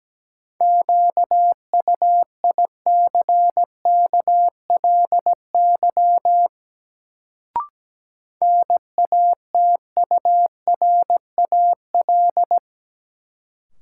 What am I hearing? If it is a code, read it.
Morse code, heard as a keyed tone, about 17 wpm, 700 Hz. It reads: QUICKLY NATURAL
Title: Sentences from Top 700 Words - New Word Drill - 17wpm